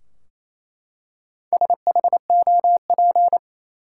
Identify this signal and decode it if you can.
Morse code, heard as a keyed tone, about 28 wpm, 700 Hz. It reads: SHOP
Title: Four Letter Words - 28wpm